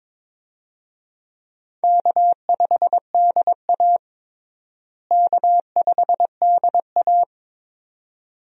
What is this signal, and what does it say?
Morse code, heard as a keyed tone, about 22 wpm, 700 Hz. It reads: K5DA K5DA
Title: USA Call Signs - 4 Characters - Repeated 2x - 22wpm